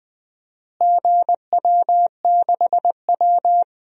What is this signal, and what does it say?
Morse code, heard as a keyed tone, about 20 wpm, 700 Hz. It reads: GW6W
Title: DXCC-100 Call Signs - 4 Characters - Repeated 2x - 20wpm